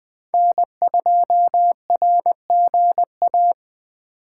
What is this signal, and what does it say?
Morse code, heard as a keyed tone, about 20 wpm, 700 Hz. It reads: N2RGA
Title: USA Call Signs - 5 Characters - 6x Character Spacing - 20wpm